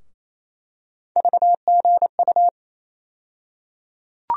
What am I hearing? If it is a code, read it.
Morse code, heard as a keyed tone, about 28 wpm, 700 Hz. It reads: VGU
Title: USA Call Sign Postfixes - 3 Letters 28wpm